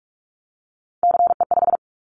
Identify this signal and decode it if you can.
Morse code, heard as a keyed tone, about 46 wpm, 700 Hz. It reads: CE5
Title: Callsign Prefixes for Top 100 DXCC - 45wpm